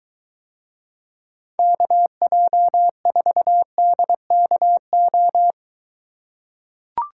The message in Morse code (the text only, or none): KJ4DKO